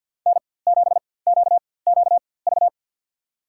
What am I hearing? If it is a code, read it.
Morse code, heard as a keyed tone, about 46 wpm, spaced as for 23 wpm, 700 Hz. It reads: N7QQV
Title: USA Call Signs - 5 Characters - 4x Character Spacing - 50wpm